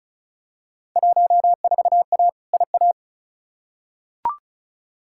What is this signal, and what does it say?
Morse code, heard as a keyed tone, about 36 wpm, 700 Hz. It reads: 14A IA
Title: Field Day Exchanges Encore - 35wpm